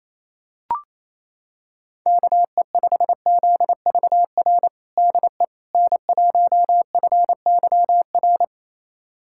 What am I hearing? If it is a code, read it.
Morse code, heard as a keyed tone, about 27 wpm, 700 Hz. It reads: KE5ZVR DE N1FYR
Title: USA Callsign DE Callsign 28wpm